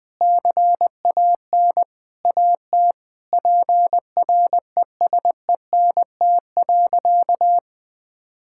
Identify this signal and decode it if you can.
Morse code, heard as a keyed tone, about 20 wpm, 700 Hz. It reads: CAN AT PRESENT.
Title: Sentences from Top 1000 Words - Review Part 5 - 20wpm